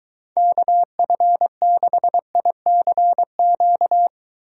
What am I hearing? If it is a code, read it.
Morse code, heard as a keyed tone, about 22 wpm, 700 Hz. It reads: KF6ICQ